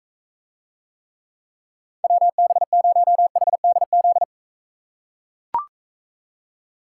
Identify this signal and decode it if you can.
Morse code, heard as a keyed tone, about 42 wpm, 700 Hz. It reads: WB0HDZ